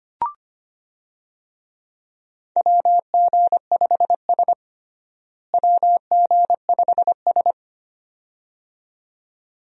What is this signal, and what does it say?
Morse code, heard as a keyed tone, about 25 wpm, 700 Hz. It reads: WG5S WG5S
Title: USA Call Signs - 4 Characters - Repeated 2x - 25wpm